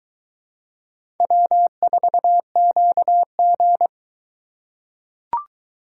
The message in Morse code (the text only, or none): W4QG